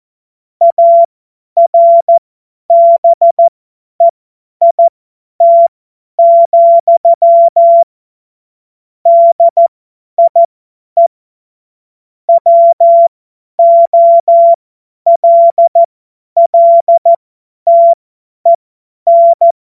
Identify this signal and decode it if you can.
Morse code, heard as a keyed tone, about 14 wpm, spaced as for 10 wpm, 680 Hz. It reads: ARBEIT, DIE WOLLTEN
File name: Morsekurs / Witze / Witz 03. Mai 2022 50 F70